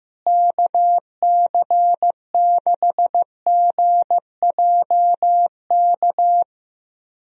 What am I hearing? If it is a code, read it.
Morse code, heard as a keyed tone, about 16 wpm, 700 Hz. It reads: KC6GJK
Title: USA Call Signs - 6 Characters - Repeated 2x - 15wpm